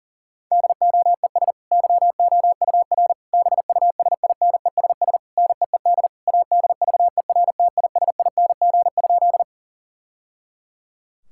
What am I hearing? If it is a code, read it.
Morse code, heard as a keyed tone, about 40 wpm, 700 Hz. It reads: DOES YOUR BUSINESS NEED ADVERTISING?
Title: Sentences from Top 1000 Words - Review Part 6 - 40wpm